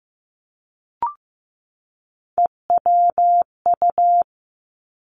Morse code, heard as a keyed tone, about 15 wpm, 700 Hz. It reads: EWU
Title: Three Challenging Letter-Number Combinations 15wpm